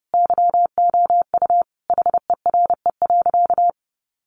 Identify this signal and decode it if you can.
Morse code, heard as a keyed tone, about 30 wpm, 700 Hz. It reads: YOU HERE.
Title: A Sound Of Thunder 30wpm